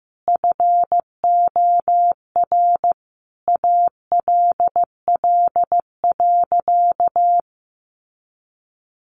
Morse code, heard as a keyed tone, about 15 wpm, 700 Hz. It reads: FOR ALL.
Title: Sentences from Top 1500 Words 15wpm